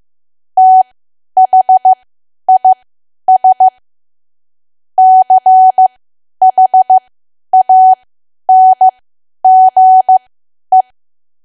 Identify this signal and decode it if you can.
Morse code, heard as a keyed tone, about 15 wpm, spaced as for 10 wpm, 750 Hz. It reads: THIS CHANGE
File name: cwa / W1AW-Level2 / 10WPM